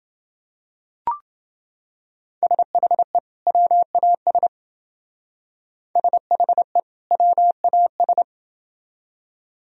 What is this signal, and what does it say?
Morse code, heard as a keyed tone, about 28 wpm, 700 Hz. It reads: SHE WAS SHE WAS